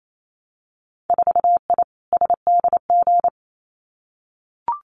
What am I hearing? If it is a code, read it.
Morse code, heard as a keyed tone, about 28 wpm, 700 Hz. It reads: 4I SDG
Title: Winter Field Day Exchanges Encore 28wpm